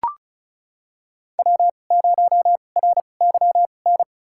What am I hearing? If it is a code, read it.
Morse code, heard as a keyed tone, about 36 wpm, spaced as for 25 wpm, 700 Hz. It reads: W0RYN
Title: USA Call Signs - 5 Characters - 2x Character Spacing - 35wpm